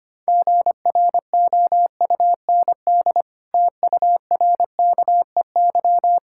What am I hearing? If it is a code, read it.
Morse code, heard as a keyed tone, about 25 wpm, 700 Hz. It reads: GROUND TURKEY
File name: practice / 25wpm / Difficult Words with Hints - 25wpm